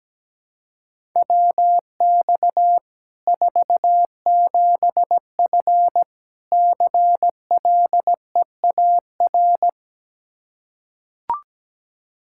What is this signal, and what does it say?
Morse code, heard as a keyed tone, about 17 wpm, 700 Hz. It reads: WX 47F CLEAR
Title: Practice QSO - Exchange 2 - Info - 17wpm